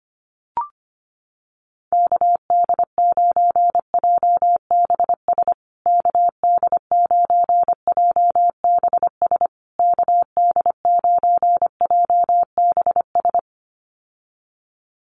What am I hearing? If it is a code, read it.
Morse code, heard as a keyed tone, about 25 wpm, 700 Hz. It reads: KD9JBS KD9JBS KD9JBS
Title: USA Call Signs Repeated 3x 25wpm